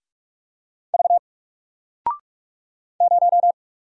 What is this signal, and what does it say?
Morse code, heard as a keyed tone, about 44 wpm, 700 Hz. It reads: V 0
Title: Single Character (Letter:Number:Punctuation:Prosign) 45wpm